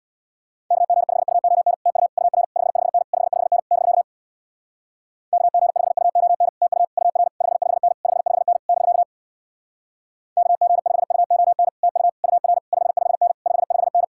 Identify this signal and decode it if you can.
Morse code, heard as a keyed tone, about 64 wpm, 700 Hz. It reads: KG5UON TU UR 55N 55N <BK> KG5UON TU UR 55N 55N <BK> KG5UON TU UR 55N 55N